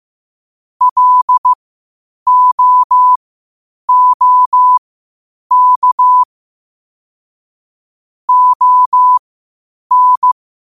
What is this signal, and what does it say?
Morse code, heard as a keyed tone, about 15 wpm, spaced as for 8 wpm, 1000 Hz. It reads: LOOK ON
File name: qotdfiles / 8wpm